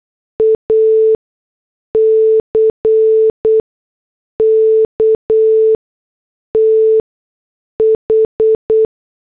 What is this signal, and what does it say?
Morse code, heard as a keyed tone, about 8 wpm, spaced as for 6 wpm, 440 Hz. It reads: ACK TH
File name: morse / headlines